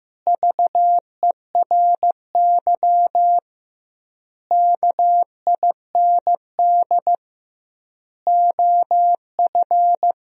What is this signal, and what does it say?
Morse code, heard as a keyed tone, about 15 wpm, 700 Hz. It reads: VERY KIND OF
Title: 2x Word-Spacing - Sentences from Top 200 Words - 15wpm